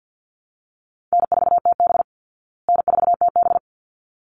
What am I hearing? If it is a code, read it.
Morse code, heard as a keyed tone, about 50 wpm, 700 Hz. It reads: N4TB N4TB
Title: Easy Call Signs Repeated 2x - 50wpm